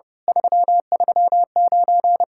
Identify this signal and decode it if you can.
Morse code, heard as a keyed tone, about 30 wpm, 700 Hz. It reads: 339
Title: Speed-Racing Common QSO Phrases - 30wpm